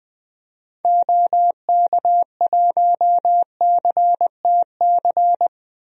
Morse code, heard as a keyed tone, about 20 wpm, 700 Hz. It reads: OK1CTC
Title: DXCC-100 Call Signs - 6 Characters - 20wpm